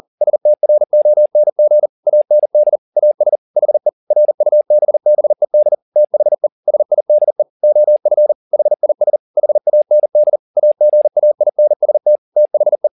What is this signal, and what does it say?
Morse code, heard as a keyed tone, about 40 wpm, 600 Hz. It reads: STRONG AND AS HE RUBBED THE SIDE OF HIS HAND AGAINST THE
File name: the old man and the sea 2 40wpm